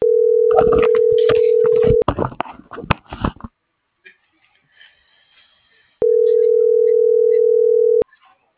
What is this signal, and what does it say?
Ringback tone (North America), 440 + 480 Hz.